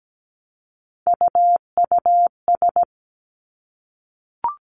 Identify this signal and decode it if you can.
Morse code, heard as a keyed tone, about 17 wpm, 700 Hz. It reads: UUS
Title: Three Challenging Letter-Number Combinations 17wpm